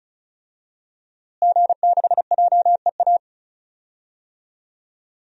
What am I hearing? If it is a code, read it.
Morse code, heard as a keyed tone, about 36 wpm, 700 Hz. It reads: G6JEA
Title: DXCC-100 Call Signs - 5 Characters - Repeated 3x - 35wpm